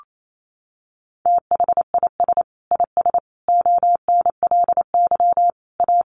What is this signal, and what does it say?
Morse code, heard as a keyed tone, about 28 wpm, 700 Hz. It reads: THIS IS ONLY A
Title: Sentences from Top 1400 Words 28wpm